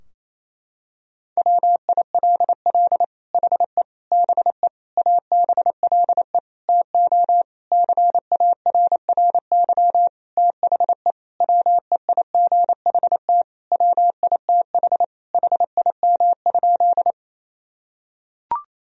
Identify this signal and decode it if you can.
Morse code, heard as a keyed tone, about 28 wpm, 700 Hz. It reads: WILL HE BE ABLE TO CARRY THE WEIGHT WITH HIM?
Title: Sentences from Top 1000 Words - Review Part 3 - 28wpm